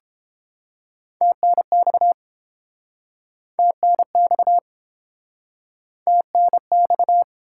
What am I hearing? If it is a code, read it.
Morse code, heard as a keyed tone, about 31 wpm, 700 Hz. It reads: TNX TNX TNX